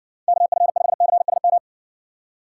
Morse code, heard as a keyed tone, about 60 wpm, 700 Hz. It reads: KF5CSM